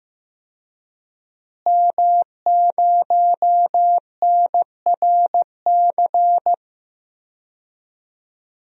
Morse code, heard as a keyed tone, about 15 wpm, 700 Hz. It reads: M0NRC
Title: DXCC-100 Call Signs - 5 Characters - Repeated 3x - 15wpm